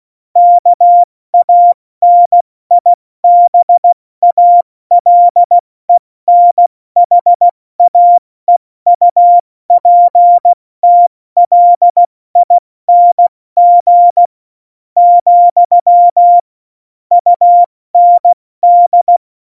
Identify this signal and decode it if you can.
Morse code, heard as a keyed tone, about 16 wpm, 700 Hz. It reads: KANIBALENHAEUPTLING , UND